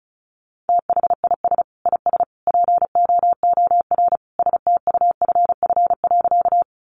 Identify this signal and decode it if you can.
Morse code, heard as a keyed tone, about 34 wpm, 700 Hz. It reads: THIS IS POOR STUFF.
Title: Sentences from Top 1400 Words 35wpm